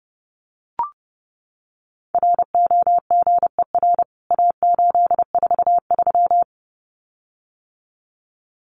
Morse code, heard as a keyed tone, about 30 wpm, 700 Hz. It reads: ROGER A843
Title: CWT Exchange 30wpm